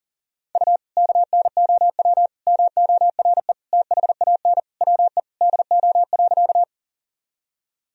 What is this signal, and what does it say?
Morse code, heard as a keyed tone, about 40 wpm, 700 Hz. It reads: U KNOW MORE THAN WE DO.
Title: Sentences from Top 100 Words and Abbreviations 40wpm